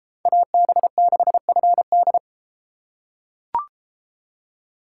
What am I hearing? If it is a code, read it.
Morse code, heard as a keyed tone, about 31 wpm, 700 Hz. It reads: AB6FD